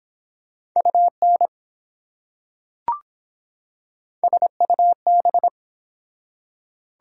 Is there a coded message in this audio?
Yes: Morse code from 0.8 to 5.5 s, heard as a keyed tone, about 24 wpm, 700 Hz. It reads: UN SUB